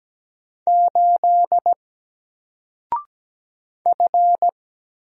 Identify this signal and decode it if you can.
Morse code, heard as a keyed tone, about 17 wpm, 700 Hz. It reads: 8 F
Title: Single Letter-Number 17wpm